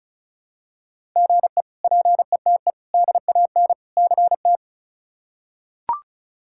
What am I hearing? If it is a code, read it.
Morse code, heard as a keyed tone, about 36 wpm, 700 Hz. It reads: GE PETE DAN CT
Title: K1USN SST QSO Practice - 35wpm